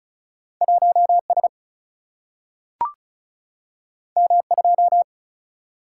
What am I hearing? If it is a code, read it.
Morse code, heard as a keyed tone, about 36 wpm, 700 Hz. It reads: 1S M2
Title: Single-Letter Single-Number 35wpm